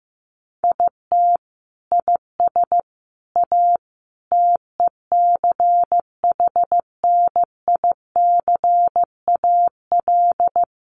Morse code, heard as a keyed tone, about 15 wpm, 700 Hz. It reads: IT IS A TECHNICAL
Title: Sentences from Top 1100 Words 15wpm